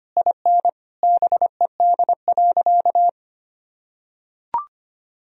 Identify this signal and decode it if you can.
Morse code, heard as a keyed tone, about 25 wpm, 700 Hz. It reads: IN BED.
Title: Sentences from Top 1000 Words - Review Part 1 - 25wpm